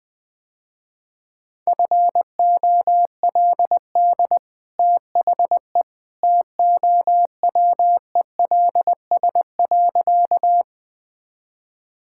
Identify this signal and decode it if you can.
Morse code, heard as a keyed tone, about 21 wpm, 700 Hz. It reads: FOLD THE TOWELS.
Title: Easy Sentences with Hints - 20wpm